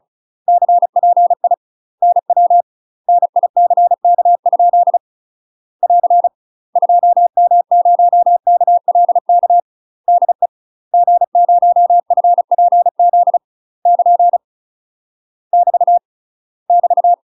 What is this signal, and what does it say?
Morse code, heard as a keyed tone, about 36 wpm, 700 Hz. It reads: CPI NW NICK? <AR> 2M0KLK DE G0FPZ <KN> <BT> <BT>
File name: UK MASTER QSOs TRAINING / 35WPM / UK MASTER QSOs 0009